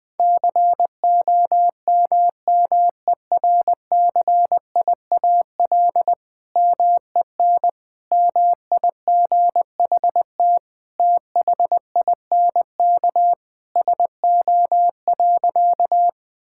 Morse code, heard as a keyed tone, about 20 wpm, 700 Hz. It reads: COMMERCIAL MEN MIGHT THINK SO.
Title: Sentences from Top 1000 Words - Review Part 5 - 20wpm